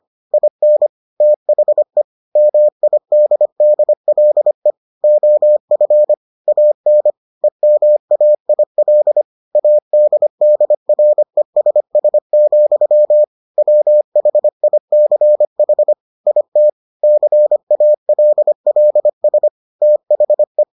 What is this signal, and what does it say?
Morse code, heard as a keyed tone, about 25 wpm, 600 Hz. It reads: IN THE MIDDLE OF AN EMAIL ADDRESS, WHICH IT CALLS THE